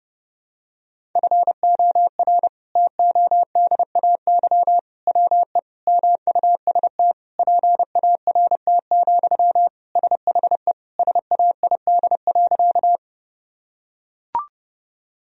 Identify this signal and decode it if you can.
Morse code, heard as a keyed tone, about 30 wpm, 700 Hz. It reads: FOR TODAY WE MUST PART, SHE SAID.
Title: Sentences from Top 1000 Words - Review Part 1 - 30wpm